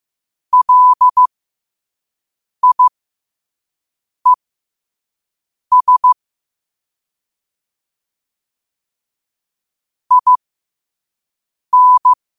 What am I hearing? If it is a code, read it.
Morse code, heard as a keyed tone, about 15 wpm, spaced as for 5 wpm, 1000 Hz. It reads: LIES IN
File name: qotdfiles / 5wpm